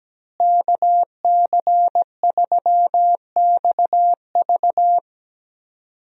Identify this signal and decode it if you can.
Morse code, heard as a keyed tone, about 17 wpm, 700 Hz. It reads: KC3XV
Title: USA Call Signs - 5 Characters - Repeated 2x - 17wpm